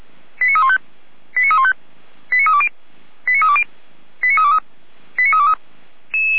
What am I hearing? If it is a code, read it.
Five-tone selective call (ZVEI), heard as quick runs of five tones, tones of about 70 ms. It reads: 89315 89315 89320 89320 89323 89323
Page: Server at audio.f85360.feuerwehr-orlamuende.org